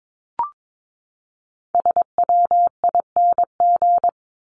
Morse code, heard as a keyed tone, about 22 wpm, 700 Hz. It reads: SWING
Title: Top 1751-2000 Words 22wpm